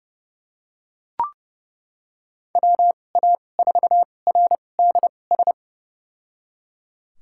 Morse code, heard as a keyed tone, about 30 wpm, spaced as for 22 wpm, 700 Hz. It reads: WA4RDS
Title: USA Call Signs - 2x Character Spacing - 30wpm